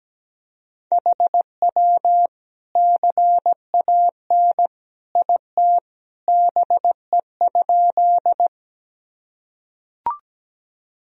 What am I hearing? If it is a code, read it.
Morse code, heard as a keyed tone, about 17 wpm, 700 Hz. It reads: HW CAN IT BE?
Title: Sentences from Top 100 Words and Abbreviations 17wpm